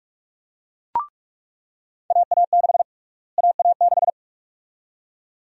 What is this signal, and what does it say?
Morse code, heard as a keyed tone, about 42 wpm, 700 Hz. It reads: AA6 AA6
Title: USA Call Sign Prefixes - Repeated 2x - 45wpm